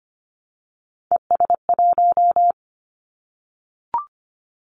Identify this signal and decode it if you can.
Morse code, heard as a keyed tone, about 25 wpm, 700 Hz. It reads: ES1
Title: Callsign Prefixes for Top 100 DXCC - 25wpm